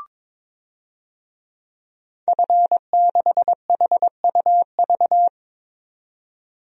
Morse code, heard as a keyed tone, about 22 wpm, 700 Hz. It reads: F6HUV
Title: DXCC-100 Call Signs - 5 Characters - 22wpm